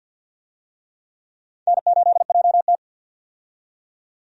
Morse code, heard as a keyed tone, about 47 wpm, 700 Hz. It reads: N8JT
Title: USA Call Signs - 4 Characters - 50wpm